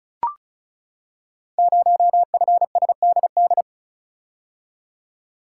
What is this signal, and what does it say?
Morse code, heard as a keyed tone, about 35 wpm, 690 Hz. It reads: 0FSDD
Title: Code Groups - 5 Letter:Numbers - 35wpm